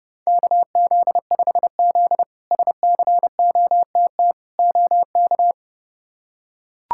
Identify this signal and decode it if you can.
Morse code, heard as a keyed tone, about 30 wpm, 700 Hz. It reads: KZ5Z SCOTT OK
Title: K1USN SST QSO Practice - 30wpm